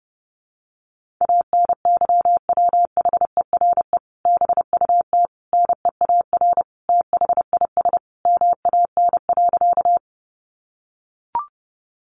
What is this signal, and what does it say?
Morse code, heard as a keyed tone, about 30 wpm, 700 Hz. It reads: ANYWHERE BUT NEAR THIS MAN.
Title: Sentences from Top 1400 Words 30wpm